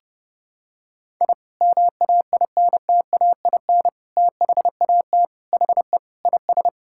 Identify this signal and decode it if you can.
Morse code, heard as a keyed tone, about 32 wpm, 700 Hz. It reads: I MAINTAIN THAT HE IS
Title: Sentences from Top 1000 Words - Review Part 5 - 30wpm